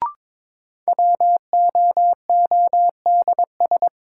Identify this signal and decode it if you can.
Morse code, heard as a keyed tone, about 22 wpm, 700 Hz. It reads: WOODS
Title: Sentences from Top 1000 Words - Review Part 2 - New Word Drill - 22wpm